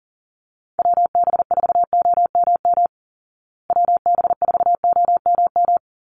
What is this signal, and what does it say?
Morse code, heard as a keyed tone, about 37 wpm, 700 Hz. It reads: WB4OMM WB4OMM
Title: Call Signs Repeated 2x - 40wpm